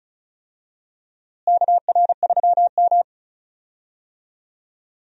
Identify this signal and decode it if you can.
Morse code, heard as a keyed tone, about 34 wpm, 700 Hz. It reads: KR3M
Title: USA Call Signs - 4 Characters - Repeated 3x - 35wpm